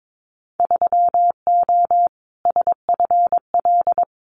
Morse code, heard as a keyed tone, about 22 wpm, 700 Hz. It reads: 3O SFL
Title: Winter Field Day Exchanges 22wpm